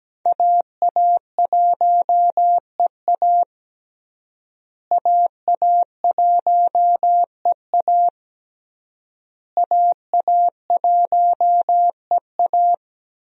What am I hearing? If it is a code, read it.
Morse code, heard as a keyed tone, about 17 wpm, 700 Hz. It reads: AA1EA AA1EA AA1EA
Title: USA Call Signs - 5 Characters - Repeated 3x - 17wpm